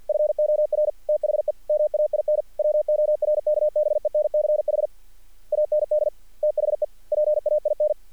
Morse code, heard as a keyed tone, about 50 wpm, 600 Hz. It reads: FOR THE MAIN WORKBENCH AND THE PAIN
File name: archived / CRSnet